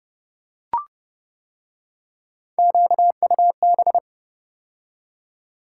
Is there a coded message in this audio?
Yes: Morse code from 2.6 to 4.0 s, heard as a keyed tone, about 29 wpm, 700 Hz. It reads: QUB